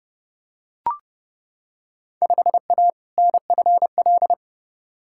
Morse code, heard as a keyed tone, about 30 wpm, 700 Hz. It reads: 5A NFL
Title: Field Day Exchanges - 30wpm